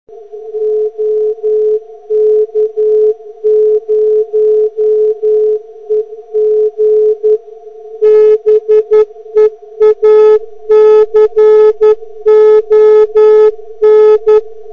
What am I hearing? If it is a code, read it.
Morse code, heard as a keyed tone, about 11 wpm, 430 Hz. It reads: OK0EG BEACON